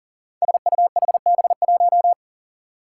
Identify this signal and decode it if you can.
Morse code, heard as a keyed tone, about 40 wpm, 700 Hz. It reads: SUHB1
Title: Five Challenging Letter-Number Combinations 40wpm